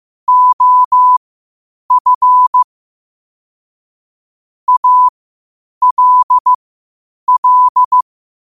Morse code, heard as a keyed tone, about 15 wpm, spaced as for 8 wpm, 1000 Hz. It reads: OF ALL
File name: qotdfiles / 8wpm